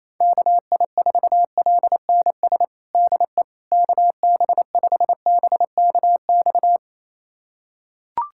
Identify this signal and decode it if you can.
Morse code, heard as a keyed tone, about 28 wpm, 700 Hz. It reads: KI4LNS DE KB5BKX
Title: USA Callsign DE Callsign 28wpm